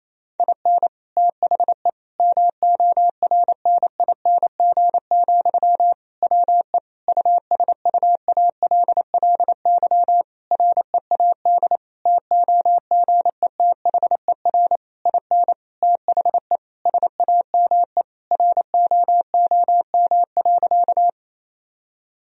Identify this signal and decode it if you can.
Morse code, heard as a keyed tone, about 28 wpm, 700 Hz. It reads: IN THE MORNING, WE USUALLY READ TOGETHER IN THE SAME ROOM.
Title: Sentences from Top 1000 Words - Review Part 3 - 28wpm